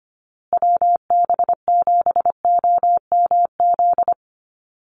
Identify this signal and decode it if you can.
Morse code, heard as a keyed tone, about 25 wpm, 700 Hz. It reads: WB7OMZ
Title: USA Call Signs Repeated 2x 25wpm